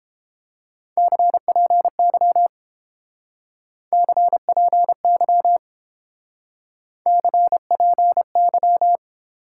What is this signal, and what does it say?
Morse code, heard as a keyed tone, about 28 wpm, 700 Hz. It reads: CPY CPY CPY
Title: Speed-Racing Top 180 QSO Elements 22wpm